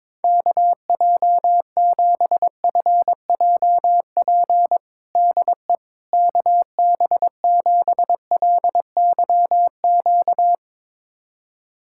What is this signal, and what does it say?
Morse code, heard as a keyed tone, about 22 wpm, 700 Hz. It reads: KJ7FJP DE KB7LYQ
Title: USA Callsign DE Callsign - 6 Characters - 22wpm